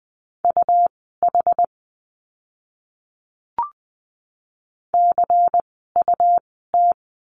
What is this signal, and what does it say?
Morse code, heard as a keyed tone, about 20 wpm, spaced as for 14 wpm, 700 Hz. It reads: UH CUT
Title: 101 Lesson 17f - Letter u - Words 20wpm